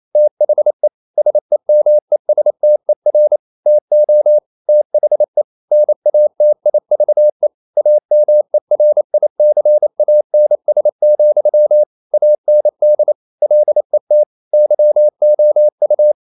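Morse code, heard as a keyed tone, about 28 wpm, 600 Hz. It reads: THE SEMESTER TO THE NATIVE AMERICANS, AND LET YOU
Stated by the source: Morse code practice at 28 wpm, text of Ken Follett's novel Never in English, from CW med Gnister.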